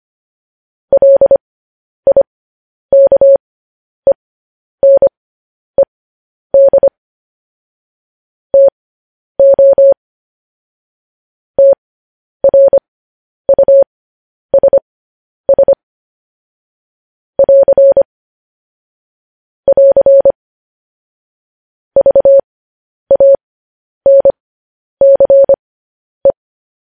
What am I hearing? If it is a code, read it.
Morse code, heard as a keyed tone, about 25 wpm, spaced as for 10 wpm, 550 Hz. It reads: LIKENED TO TRUSS <AR> <AR> VANCE
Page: news-headlines-in-morse-code-at-10-wpm